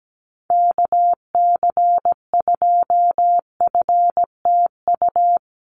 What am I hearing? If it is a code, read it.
Morse code, heard as a keyed tone, about 17 wpm, 700 Hz. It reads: KC2FTU
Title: USA Call Signs Repeated 2x 17wpm